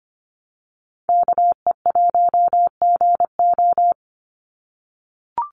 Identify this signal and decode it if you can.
Morse code, heard as a keyed tone, about 25 wpm, 700 Hz. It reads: KE1GO
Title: USA Call Signs 25wpm